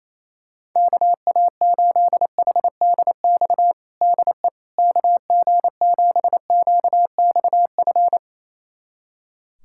Morse code, heard as a keyed tone, about 28 wpm, 700 Hz. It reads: KA8HDX DE KG7QXF
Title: USA Callsign DE Callsign - 6 Characters - 28wpm